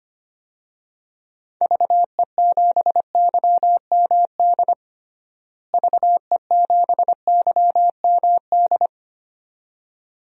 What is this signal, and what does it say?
Morse code, heard as a keyed tone, about 25 wpm, 700 Hz. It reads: VE7YMD VE7YMD
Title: DXCC-100 Call Signs - 6 Characters - Repeated 2x - 25wpm